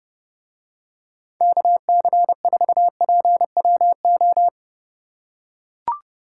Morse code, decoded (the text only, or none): KC4PWO